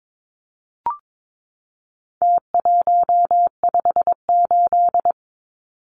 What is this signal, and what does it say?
Morse code, heard as a keyed tone, about 22 wpm, 700 Hz. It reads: T158
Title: Numbers - Contest Style - 4-Digit 22wpm